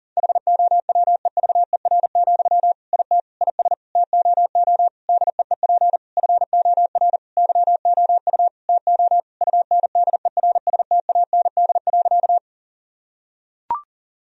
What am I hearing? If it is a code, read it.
Morse code, heard as a keyed tone, about 37 wpm, 700 Hz. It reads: HOWEVER, IT IS TOO DEEP FOR YOU TO UNDERSTAND.
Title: Sentences from Top 1000 Words - Review Part 3 - 40wpm